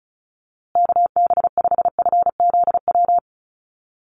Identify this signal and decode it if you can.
Morse code, heard as a keyed tone, about 34 wpm, 700 Hz. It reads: KB5FZW
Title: USA Call Signs Repeated 3x 35wpm